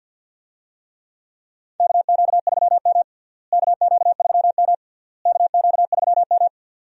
Morse code, heard as a keyed tone, about 46 wpm, 700 Hz. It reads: KQ3M KQ3M KQ3M
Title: USA Call Signs - 4 Characters - Repeated 3x - 50wpm